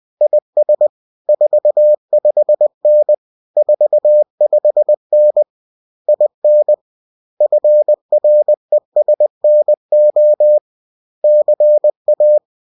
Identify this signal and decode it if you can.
Morse code, heard as a keyed tone, about 20 wpm, 600 Hz. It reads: IS 45N 45N IN FRESNO CA